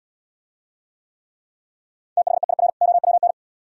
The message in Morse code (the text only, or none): THEIR OWN